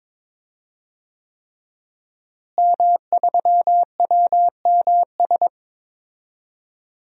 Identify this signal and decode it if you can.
Morse code, heard as a keyed tone, about 22 wpm, 700 Hz. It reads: M3WMS
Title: DXCC-100 Call Signs - 5 Characters - Repeated 3x - 22wpm